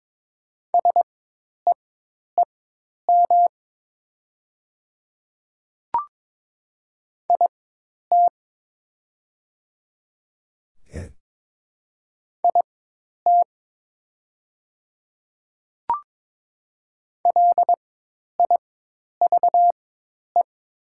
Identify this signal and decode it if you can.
Morse code, heard as a keyed tone, about 22 wpm, spaced as for 10 wpm, 700 Hz. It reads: SEEM IT IT LIVE
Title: 4x Character Spacing - Top 300 Words - 22wpm